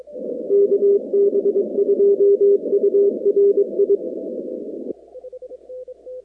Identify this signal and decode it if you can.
Morse code, heard as a keyed tone, about 22 wpm, 400 Hz. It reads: KB2URI
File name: contests / 2019-cw-wpx-kt5j-k5pi- -2330 0-46